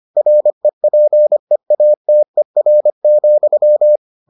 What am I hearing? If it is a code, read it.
Morse code, heard as a keyed tone, about 25 wpm, 600 Hz. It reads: REPEATER,